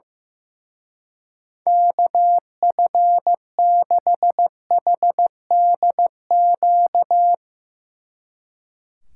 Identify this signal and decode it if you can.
Morse code, heard as a keyed tone, about 15 wpm, 700 Hz. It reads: KF6HDQ
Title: Speed Racing - USA 2x3 Call Signs 15wpm